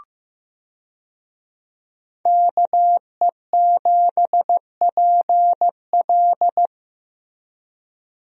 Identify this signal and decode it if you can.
Morse code, heard as a keyed tone, about 15 wpm, 700 Hz. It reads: KE7PL
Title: USA Call Signs - 5 Characters - Repeated 2x - 15wpm